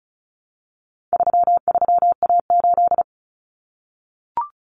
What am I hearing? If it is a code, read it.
Morse code, heard as a keyed tone, about 34 wpm, 700 Hz. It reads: 33A8
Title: Common Cut Numbers - 4-Digit 35wpm